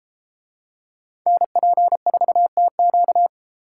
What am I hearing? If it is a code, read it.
Morse code, heard as a keyed tone, about 32 wpm, 700 Hz. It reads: NP4TQ